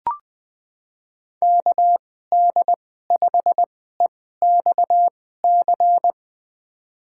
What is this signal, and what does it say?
Morse code, heard as a keyed tone, about 20 wpm, spaced as for 14 wpm, 700 Hz. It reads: KD5EXC
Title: USA Call Signs - 2x Character Spacing - 20wpm